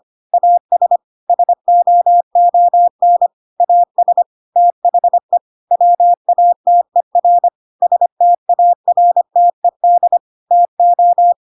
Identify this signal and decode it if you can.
Morse code, heard as a keyed tone, about 25 wpm, 700 Hz. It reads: AS SOON AS THE WATER STARTED TO
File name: Pinocchio / Pinocchio ch5 25 25 ik1hge